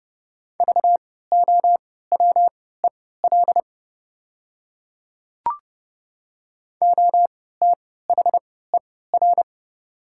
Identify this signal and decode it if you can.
Morse code, heard as a keyed tone, about 30 wpm, spaced as for 17 wpm, 690 Hz. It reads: VOWEL OTHER
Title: 3x Character Spacing - Top 500 Words - 30wpm